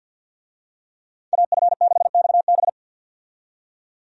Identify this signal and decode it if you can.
Morse code, heard as a keyed tone, about 52 wpm, 700 Hz. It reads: AF6XB
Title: USA Call Signs - 5 Characters - 50wpm